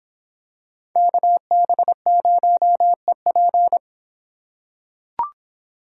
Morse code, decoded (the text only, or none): KB0EP